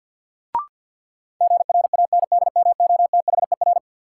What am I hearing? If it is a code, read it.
Morse code, heard as a keyed tone, about 53 wpm, 700 Hz. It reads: GRANDMOTHER
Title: Sentences from Top 1200 Words - New Word Drill Alt - 50wpm